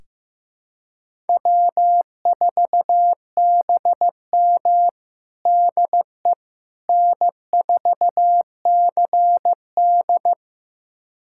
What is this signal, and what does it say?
Morse code, heard as a keyed tone, about 16 wpm, 700 Hz. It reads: W4BM DE N4CD
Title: USA Callsign DE Callsign - 4 Characters - 15wpm